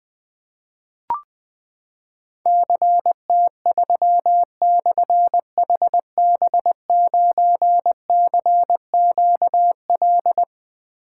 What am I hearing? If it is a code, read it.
Morse code, heard as a keyed tone, about 20 wpm, 700 Hz. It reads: CT3/HB9CQL
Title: Difficult Call Signs 20wpm